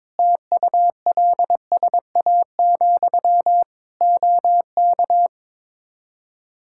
Morse code, heard as a keyed tone, about 22 wpm, 700 Hz. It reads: TULSA, OK
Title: Practice QSO - Exchange 1 - Beginning - 22wpm